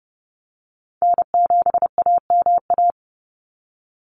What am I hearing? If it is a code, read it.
Morse code, heard as a keyed tone, about 30 wpm, 700 Hz. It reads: N7AMA
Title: NA SOTA Chaser Callsigns 30wpm